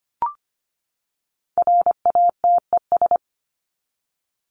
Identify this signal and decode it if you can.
Morse code, heard as a keyed tone, about 25 wpm, 700 Hz. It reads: RATES
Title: ES:S Suffix - Top 100 Words with Base Words 25wpm